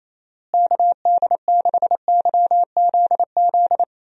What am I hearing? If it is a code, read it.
Morse code, heard as a keyed tone, about 28 wpm, 700 Hz. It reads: KD6YZZ
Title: USA Call Signs - 6 Characters - Repeated 3x - 28wpm